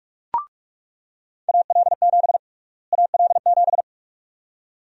Morse code, heard as a keyed tone, about 45 wpm, 700 Hz. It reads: AL7 AL7
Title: USA Call Sign Prefixes - Repeated 2x - 45wpm